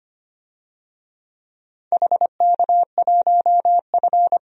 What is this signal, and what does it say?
Morse code, heard as a keyed tone, about 25 wpm, 700 Hz. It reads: HK1F
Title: DXCC-100 Call Signs - 4 Characters - 25wpm